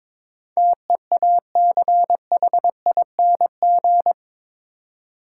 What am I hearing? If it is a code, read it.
Morse code, heard as a keyed tone, about 21 wpm, 700 Hz. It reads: TEACHING
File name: practice / 22wpm / Eight Letter Words - 22wpm